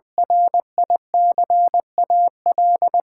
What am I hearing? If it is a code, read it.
Morse code, heard as a keyed tone, about 21 wpm, 700 Hz. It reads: RICAL
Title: Sentences from Top 1000 Words - Review Part 6 - 20wpm